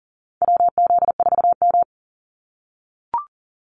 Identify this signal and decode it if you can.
Morse code, heard as a keyed tone, about 38 wpm, 700 Hz. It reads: WZ4M
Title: NA SOTA Activator Callsigns 40wpm